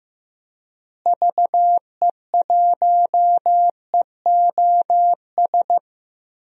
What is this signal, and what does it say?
Morse code, heard as a keyed tone, about 15 wpm, 700 Hz. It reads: VE1EOS
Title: DXCC-100 Call Signs - 6 Characters - Repeated 3x - 15wpm